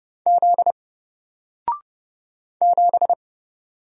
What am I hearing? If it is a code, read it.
Morse code, heard as a keyed tone, about 31 wpm, 700 Hz. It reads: Z 7
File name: Single-Letter-Number-30wpm